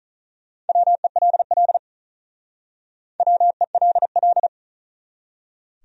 Morse code, heard as a keyed tone, about 45 wpm, 700 Hz. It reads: WELL WELL